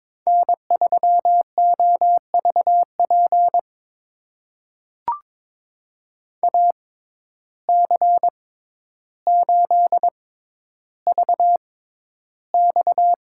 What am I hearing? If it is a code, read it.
Morse code, heard as a keyed tone, about 23 wpm, spaced as for 11 wpm, 700 Hz. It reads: N3OVP AC8VX
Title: USA Call Signs - 5 Characters - 6x Character Spacing - 22wpm